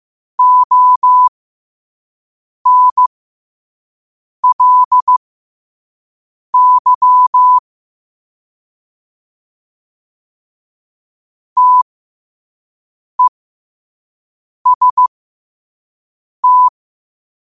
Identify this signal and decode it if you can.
Morse code, heard as a keyed tone, about 15 wpm, spaced as for 5 wpm, 1000 Hz. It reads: ONLY TEST
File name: qotdfiles / 5wpm